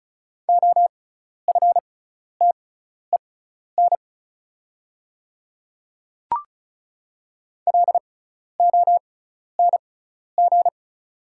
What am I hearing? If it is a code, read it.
Morse code, heard as a keyed tone, about 36 wpm, spaced as for 12 wpm, 700 Hz. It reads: OFTEN LONG
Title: 6x Character Spacing - Top 500 Words - 35wpm